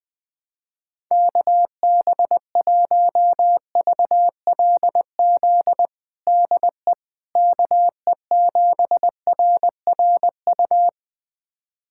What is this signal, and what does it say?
Morse code, heard as a keyed tone, about 20 wpm, 700 Hz. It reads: KB1VLZ DE KE7RRU
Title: USA Callsign DE Callsign 20wpm